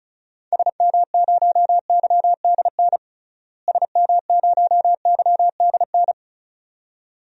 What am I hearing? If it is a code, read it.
Morse code, heard as a keyed tone, about 34 wpm, 700 Hz. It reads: SM0YDN SM0YDN
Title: DXCC-100 Call Signs - 6 Characters - Repeated 2x - 35wpm